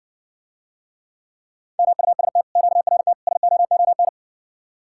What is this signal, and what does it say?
Morse code, heard as a keyed tone, about 60 wpm, 700 Hz. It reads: MUST QRT SOON